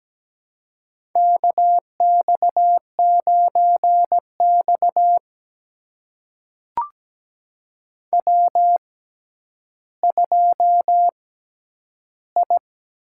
Morse code, heard as a keyed tone, about 17 wpm, spaced as for 10 wpm, 700 Hz. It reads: KX9X W2I
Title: USA Call Signs - 6x Character Spacing - 17wpm